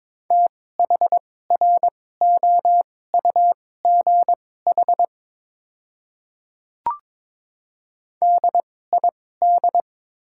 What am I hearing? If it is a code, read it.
Morse code, heard as a keyed tone, about 22 wpm, spaced as for 16 wpm, 700 Hz. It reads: THROUGH DID
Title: 2x Character Spacing - Top 200 Words - 22wpm